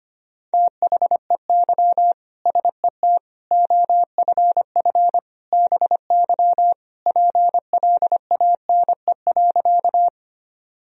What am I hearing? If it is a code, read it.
Morse code, heard as a keyed tone, about 25 wpm, 700 Hz. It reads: THEY SET OFF BY PLANE.
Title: Sentences from Top 1000 Words - Review Part 3 - 25wpm